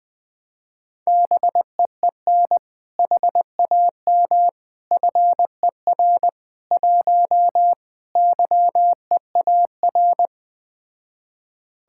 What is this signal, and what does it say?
Morse code, heard as a keyed tone, about 20 wpm, 700 Hz. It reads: BEEN HAM FER 1 YEAR
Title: Speed-Racing Common QSO Phrases - 17wpm